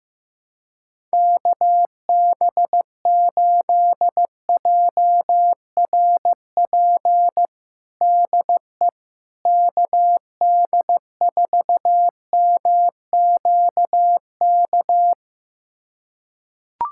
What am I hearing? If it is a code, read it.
Morse code, heard as a keyed tone, about 15 wpm, 700 Hz. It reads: KB8JRP DE KD4MQK
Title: USA Callsign DE Callsign - 6 Characters - 15wpm